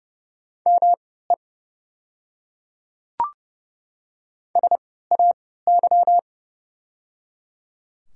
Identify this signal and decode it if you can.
Morse code, heard as a keyed tone, about 30 wpm, spaced as for 17 wpm, 690 Hz. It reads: ME SAY
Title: 3x Character Spacing - Top 300 Words - 30wpm